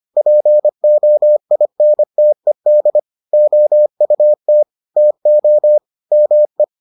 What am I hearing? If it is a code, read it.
Morse code, heard as a keyed tone, about 25 wpm, 600 Hz. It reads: POINTED OUT TO ME